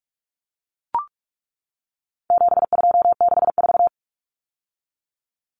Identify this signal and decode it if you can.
Morse code, heard as a keyed tone, about 44 wpm, 700 Hz. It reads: 7264
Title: Numbers - 4-Digit 45wpm